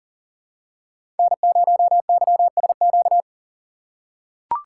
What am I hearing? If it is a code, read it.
Morse code, heard as a keyed tone, about 40 wpm, 700 Hz. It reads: N0YSQ
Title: K1USN SST QSO Practice - 40wpm